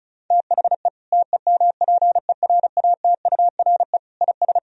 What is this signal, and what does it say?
Morse code, heard as a keyed tone, about 34 wpm, 700 Hz. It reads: THE TEMPERATURE IS
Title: Sentences from Top 1000 Words - Review Part 4 - 35wpm